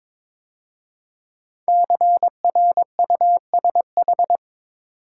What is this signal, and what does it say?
Morse code, heard as a keyed tone, about 22 wpm, 700 Hz. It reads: CRUSH
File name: 5 Letter Words - First 3 Letters Given - 22wpm